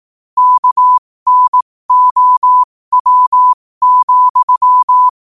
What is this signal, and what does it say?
Morse code, heard as a keyed tone, about 18 wpm, spaced as for 15 wpm, 1000 Hz. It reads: KNOW,
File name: qotdfiles / 15wpm